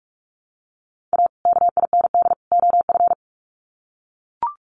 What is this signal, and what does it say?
Morse code, heard as a keyed tone, about 44 wpm, 700 Hz. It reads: A KIND OF
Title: Sets of 3 Words 45wpm